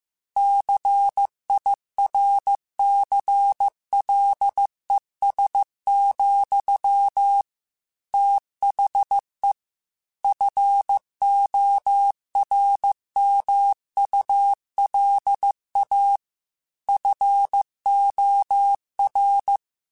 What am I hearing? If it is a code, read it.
Morse code, heard as a keyed tone, about 15 wpm, 800 Hz. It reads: CIRCLES, THE FORMULA FOR